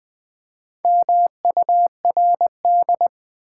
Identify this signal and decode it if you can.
Morse code, heard as a keyed tone, about 20 wpm, 690 Hz. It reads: MURD
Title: Build A Word - 2 Letters At A Time - 6-Letter Words - 20wpm